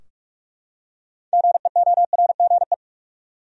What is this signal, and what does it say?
Morse code, heard as a keyed tone, about 46 wpm, 700 Hz. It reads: GEORGE
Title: USA Common Names - 45wpm